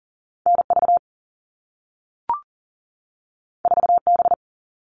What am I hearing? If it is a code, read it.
Morse code, heard as a keyed tone, about 40 wpm, 700 Hz. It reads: NV 4B
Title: All Cut Numbers - 2-Digit 40wpm